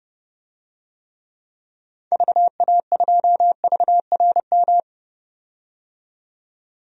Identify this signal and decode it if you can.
Morse code, heard as a keyed tone, about 30 wpm, 700 Hz. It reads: VA2VRM
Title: DXCC-100 Call Signs - 6 Characters - 30wpm